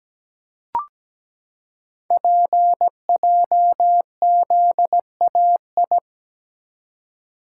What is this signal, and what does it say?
Morse code, heard as a keyed tone, about 17 wpm, 700 Hz. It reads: PJZAI
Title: Code Groups - 5 Letters - 17wpm